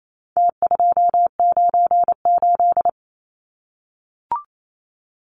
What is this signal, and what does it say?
Morse code, heard as a keyed tone, about 28 wpm, 700 Hz. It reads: T298
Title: Numbers - Contest Style - 4-Digit 28wpm